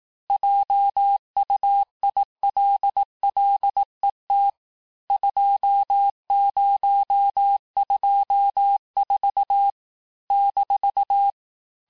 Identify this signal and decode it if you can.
Morse code, heard as a keyed tone, about 18 wpm, 800 Hz. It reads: JUILLET 2024 -
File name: cours-du-lundi files / cours-1588-texte-chiffre-vitesse-17MPM